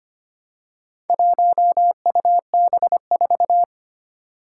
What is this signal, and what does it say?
Morse code, heard as a keyed tone, about 25 wpm, 700 Hz. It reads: 1UB4
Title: Four Challenging Letter-Number Combinations 25wpm